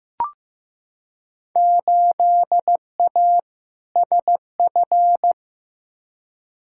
Morse code, heard as a keyed tone, about 15 wpm, 700 Hz. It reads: 8A SF
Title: Field Day Exchanges Encore - 15wpm